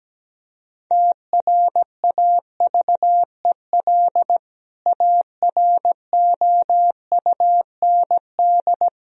Morse code, heard as a keyed tone, about 17 wpm, 700 Hz. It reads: TRAVEL AROUND
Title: Sentences from Top 1000 Words - Review Part 3 - 17wpm